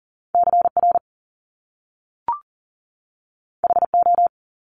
Morse code, heard as a keyed tone, about 40 wpm, 700 Hz. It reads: CR HO
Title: Top Two Letter Combinations in Words 40wpm